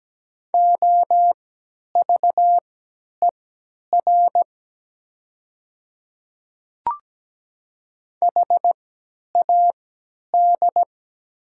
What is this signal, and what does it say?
Morse code, heard as a keyed tone, about 17 wpm, spaced as for 10 wpm, 700 Hz. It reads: OVER HAD
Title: 3x Character Spacing - Top 500 Words - 17wpm